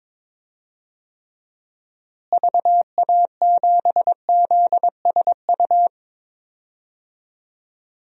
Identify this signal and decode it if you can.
Morse code, heard as a keyed tone, about 22 wpm, 700 Hz. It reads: VA7ZSU
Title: DXCC-100 Call Signs - 6 Characters - Repeated 2x - 22wpm